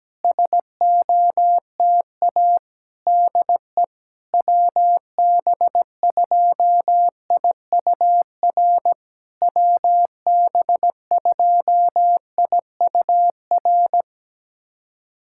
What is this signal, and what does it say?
Morse code, heard as a keyed tone, about 17 wpm, 700 Hz. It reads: SOTA DE WB2IUR WB2IUR
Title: SOTA QSO Practice 17wpm